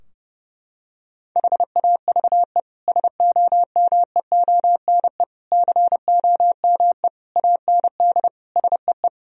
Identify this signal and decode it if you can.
Morse code, heard as a keyed tone, about 30 wpm, 700 Hz. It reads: HAVE SOMEONE COME AND SEE
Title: Sentences from Top 1000 Words - Review Part 4 - 30wpm